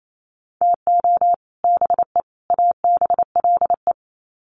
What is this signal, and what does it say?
Morse code, heard as a keyed tone, about 28 wpm, 700 Hz. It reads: TO BE ABLE
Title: Sets of 3 Words 28wpm